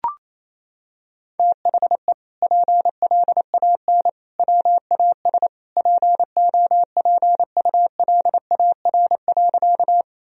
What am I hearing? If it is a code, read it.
Morse code, heard as a keyed tone, about 28 wpm, 700 Hz. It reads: THE PLAN WAS POPULAR.
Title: Sentences from Top 1000 Words - Review Part 3 - 28wpm